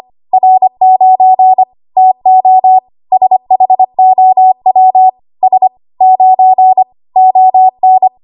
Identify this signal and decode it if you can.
Morse code, heard as a keyed tone, about 25 wpm, 750 Hz. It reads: R9 TO SHOW S 9 ON
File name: Archive / 25 WPM / 25WPM